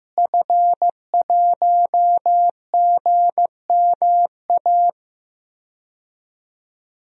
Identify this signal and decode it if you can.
Morse code, heard as a keyed tone, about 15 wpm, 700 Hz. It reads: F1GMA
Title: DXCC-100 Call Signs - 5 Characters - Repeated 3x - 15wpm